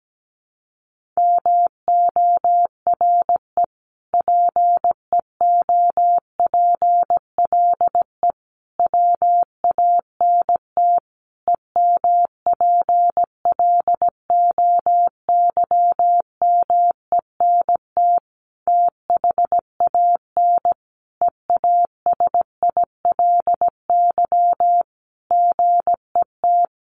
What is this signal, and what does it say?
Morse code, heard as a keyed tone, about 17 wpm, 700 Hz. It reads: MORE PEOPLE WANT EMPLOYMENT THAN EASILY GET
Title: Sentences from Top 1500 Words 17wpm